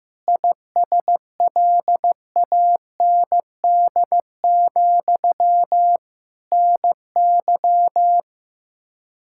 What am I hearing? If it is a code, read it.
Morse code, heard as a keyed tone, about 16 wpm, 700 Hz. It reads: ISLAND, NY
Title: Practice QSO - Exchange 1 - Beginning - 15wpm